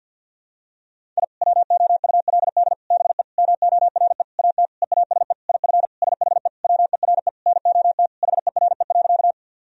Morse code, heard as a keyed tone, about 51 wpm, 700 Hz. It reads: I WOULD BE MORE AT EASE IF SHE WERE NOT HERE.
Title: Sentences from Top 1000 Words - Review Part 1 - 50wpm